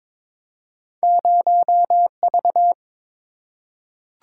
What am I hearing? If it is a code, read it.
Morse code, heard as a keyed tone, about 22 wpm, 700 Hz. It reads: 0V
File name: practice / 22wpm / Two Letter-Number Combinations - 22wpm